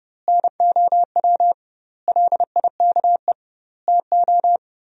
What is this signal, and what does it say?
Morse code, heard as a keyed tone, about 32 wpm, 700 Hz. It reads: NOW LIKE TO
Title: 2x Word-Spacing - Sentences from Top 200 Words - 30wpm